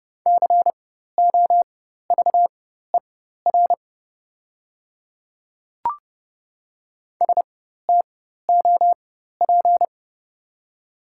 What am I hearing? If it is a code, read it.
Morse code, heard as a keyed tone, about 30 wpm, spaced as for 14 wpm, 700 Hz. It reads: COVER STOP
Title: 4x Character Spacing - Top 300 Words - 30wpm